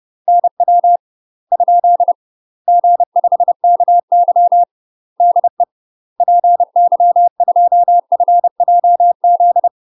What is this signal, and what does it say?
Morse code, heard as a keyed tone, about 30 wpm, 700 Hz. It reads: NW ? G5KY DE PY2FJZ
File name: UK MASTER QSOs TRAINING/30WPM